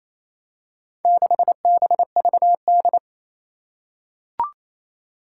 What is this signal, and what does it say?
Morse code, heard as a keyed tone, about 27 wpm, 700 Hz. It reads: 6BVD
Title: Four Challenging Letter-Number Combinations 28wpm